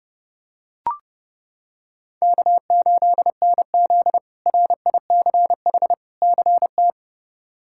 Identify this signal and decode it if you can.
Morse code, heard as a keyed tone, about 29 wpm, 700 Hz. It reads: K8NZ RICH CT
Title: K1USN SST QSO Practice - 30wpm